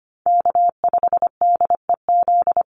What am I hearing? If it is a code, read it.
Morse code, heard as a keyed tone, about 25 wpm, 700 Hz. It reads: K5DEZ
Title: NA SOTA Chaser Callsigns 25wpm